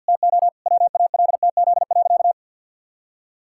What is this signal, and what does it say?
Morse code, heard as a keyed tone, about 50 wpm, 700 Hz. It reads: TO WALTZ.
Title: Pangram Challenge v2 - 50wpm